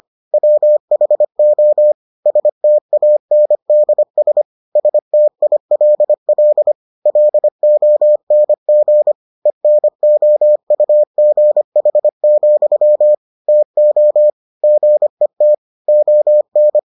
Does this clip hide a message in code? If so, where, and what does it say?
Yes: Morse code from 0.3 to 16.8 s, heard as a keyed tone, about 25 wpm, 600 Hz. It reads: WHO STANDS STILL LONG ENOUGH, TO GET ON